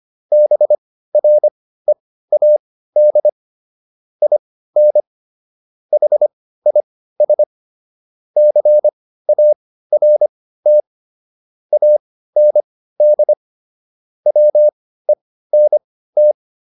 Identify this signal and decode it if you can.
Morse code, heard as a keyed tone, about 25 wpm, spaced as for 15 wpm, 600 Hz. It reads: BREAD IN HIS CART AND WENT